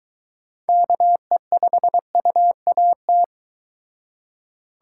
Morse code, heard as a keyed tone, about 23 wpm, 700 Hz. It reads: KE5UAT